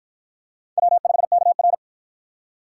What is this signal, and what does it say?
Morse code, heard as a keyed tone, about 49 wpm, 700 Hz. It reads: W5KR